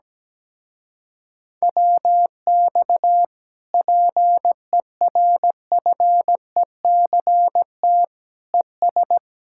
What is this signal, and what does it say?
Morse code, heard as a keyed tone, about 17 wpm, 700 Hz. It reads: WX PERFECT ES
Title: Speed-Racing Common QSO Phrases Encore - 17wpm